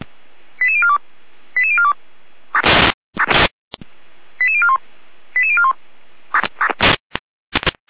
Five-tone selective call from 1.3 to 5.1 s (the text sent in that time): 88042 88041